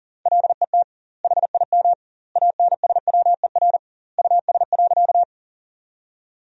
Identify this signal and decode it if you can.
Morse code, heard as a keyed tone, about 37 wpm, 700 Hz. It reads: LET HIM ANSWER US.
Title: 2x Word-Spacing - Sentences from Top 300 Words - 40wpm